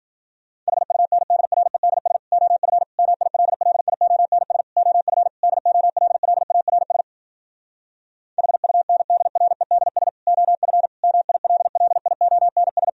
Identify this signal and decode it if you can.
Morse code, heard as a keyed tone, about 54 wpm, 700 Hz. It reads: HUNDREDS OF MILLIONS OF DOLLARS HUNDREDS OF MILLIONS